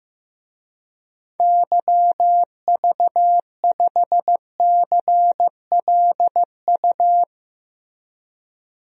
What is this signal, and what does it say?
Morse code, heard as a keyed tone, about 15 wpm, 700 Hz. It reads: YV5CLU
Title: DXCC-100 Call Signs - 6 Characters - Repeated 2x - 15wpm